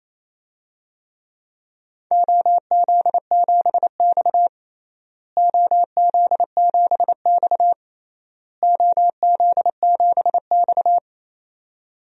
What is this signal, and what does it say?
Morse code, heard as a keyed tone, about 28 wpm, 700 Hz. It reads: OZ7X OZ7X OZ7X
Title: DXCC-100 Call Signs - 4 Characters - Repeated 3x - 28wpm